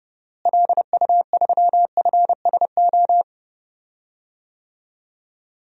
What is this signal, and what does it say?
Morse code, heard as a keyed tone, about 30 wpm, 700 Hz. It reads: LU3FSO
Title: DXCC-100 Call Signs - 6 Characters - Repeated 2x - 30wpm